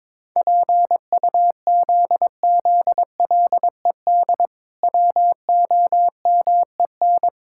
Morse code, heard as a keyed tone, about 22 wpm, 700 Hz. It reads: PUZZLED WOMEN
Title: Pangram Challenge v2 - 22wpm